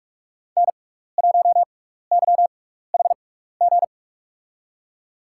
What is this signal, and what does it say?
Morse code, heard as a keyed tone, about 43 wpm, spaced as for 15 wpm, 690 Hz. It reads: N1YHG
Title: USA Call Signs - 5 Characters - 6x Character Spacing - 45wpm